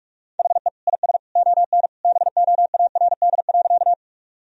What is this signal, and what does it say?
Morse code, heard as a keyed tone, about 46 wpm, 700 Hz. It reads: HE IS ON BOARD.
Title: Sentences from Top 1000 Words - Review Part 4 - 45wpm